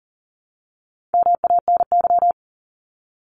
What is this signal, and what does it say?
Morse code, heard as a keyed tone, about 40 wpm, 700 Hz. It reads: MANY
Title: Top 200 Words 40wpm